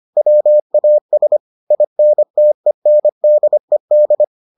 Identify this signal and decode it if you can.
Morse code, heard as a keyed tone, about 25 wpm, 600 Hz. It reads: WAS INTENDED